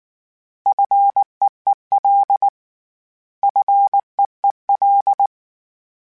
Morse code, heard as a keyed tone, about 19 wpm, 800 Hz. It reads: FEEL FEEL